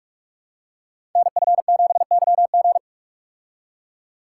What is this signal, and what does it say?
Morse code, heard as a keyed tone, about 44 wpm, 700 Hz. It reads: NF7YG
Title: Code Groups - 5 Letter:Numbers - 45wpm